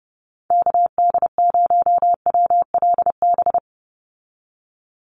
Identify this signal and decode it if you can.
Morse code, heard as a keyed tone, about 30 wpm, 700 Hz. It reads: KD0WLB
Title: USA Call Signs 30wpm